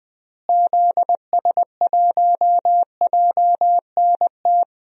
Morse code, heard as a keyed tone, about 20 wpm, 700 Hz. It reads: ZS1JNT
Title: DXCC-100 Call Signs - 6 Characters - Repeated 3x - 20wpm